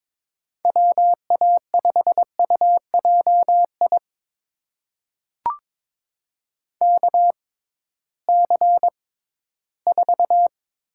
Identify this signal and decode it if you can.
Morse code, heard as a keyed tone, about 23 wpm, spaced as for 14 wpm, 700 Hz. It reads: WA5UJI KC4
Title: USA Call Signs - 6x Character Spacing - 22wpm